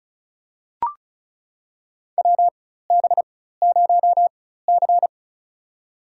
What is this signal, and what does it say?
Morse code, heard as a keyed tone, about 36 wpm, spaced as for 16 wpm, 700 Hz. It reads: WB0C
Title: USA Call Signs - 4x Character Spacing - 35wpm